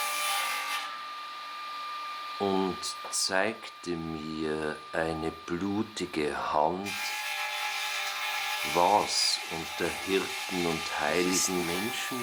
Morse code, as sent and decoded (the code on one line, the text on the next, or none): none